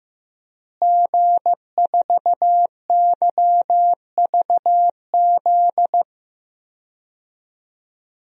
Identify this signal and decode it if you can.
Morse code, heard as a keyed tone, about 15 wpm, 700 Hz. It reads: G4YVZ
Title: DXCC-100 Call Signs - 5 Characters - Repeated 2x - 15wpm